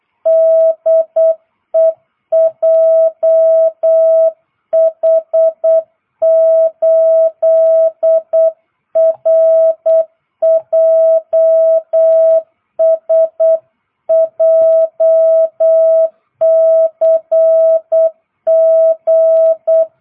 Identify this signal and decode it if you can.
Morse code, heard as a keyed tone, about 8 wpm, 650 Hz. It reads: DE JH8RJS JCG